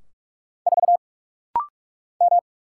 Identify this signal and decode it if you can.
Morse code, heard as a keyed tone, about 44 wpm, 700 Hz. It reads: 4 M
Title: Single Letter-Number - Rapid-Fire 45wpm